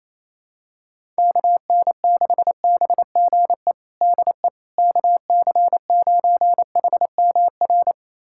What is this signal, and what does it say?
Morse code, heard as a keyed tone, about 28 wpm, 700 Hz. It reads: KN6BGE DE KC9HMR
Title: USA Callsign DE Callsign 28wpm